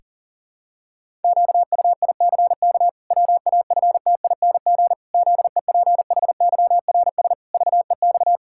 Morse code, heard as a keyed tone, about 37 wpm, 700 Hz. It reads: QUICK WAFTING ZEPHYRS VEX
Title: Pangram Challenge v2 - 40wpm